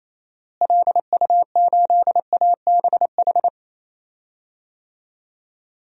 Morse code, heard as a keyed tone, about 28 wpm, 700 Hz. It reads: LU8ABH
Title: DXCC-100 Call Signs - 6 Characters - Repeated 2x - 28wpm